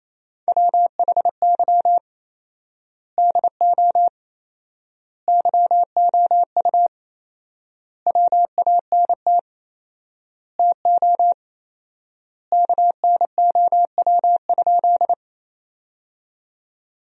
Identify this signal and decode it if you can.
Morse code, heard as a keyed tone, about 28 wpm, 700 Hz. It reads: WHY DO YOU WANT TO KNOW?
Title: 4x Word-Spacing - Sentences from Top 200 Words - 28wpm